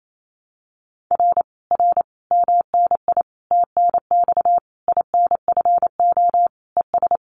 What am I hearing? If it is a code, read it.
Morse code, heard as a keyed tone, about 28 wpm, 700 Hz. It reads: R R MNI TNX INFO ES
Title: Common QSO Phrases Encore 28wpm